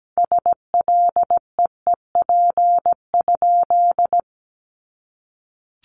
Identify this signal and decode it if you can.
Morse code, heard as a keyed tone, about 17 wpm, 700 Hz. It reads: SLEEP?
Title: Sentences from Top 500 Words 17wpm